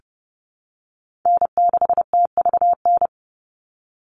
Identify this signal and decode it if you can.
Morse code, heard as a keyed tone, about 30 wpm, 700 Hz. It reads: N6TVN
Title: Call Signs - 30wpm